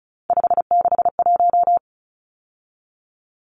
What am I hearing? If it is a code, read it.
Morse code, heard as a keyed tone, about 34 wpm, 700 Hz. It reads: 561
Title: Numbers - 3-Digit 35wpm